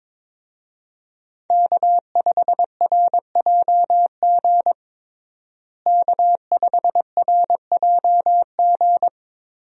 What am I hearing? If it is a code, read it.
Morse code, heard as a keyed tone, about 22 wpm, 700 Hz. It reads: K5RJG K5RJG
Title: USA Call Signs - 5 Characters - Repeated 3x - 22wpm